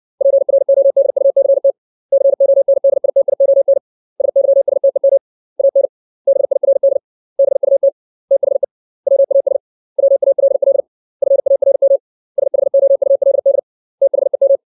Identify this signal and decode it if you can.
Morse code, heard as a keyed tone, about 60 wpm, 550 Hz. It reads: PRODUCT CONDITION SYSTEM AN BIRD BUT THE WAS WALL FARM SHOULD THEM
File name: 632 words / 632 Words 60WPM